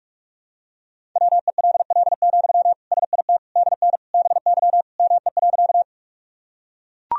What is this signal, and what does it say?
Morse code, heard as a keyed tone, about 42 wpm, 700 Hz. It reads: WELL, SIT DN BY ME.
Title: Sentences from Top 500 Words and Abbreviations 45wpm